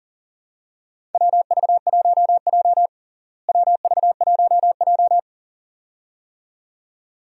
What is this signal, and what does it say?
Morse code, heard as a keyed tone, about 40 wpm, 700 Hz. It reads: WV1J WV1J
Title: USA Call Signs - 4 Characters - Repeated 2x - 40wpm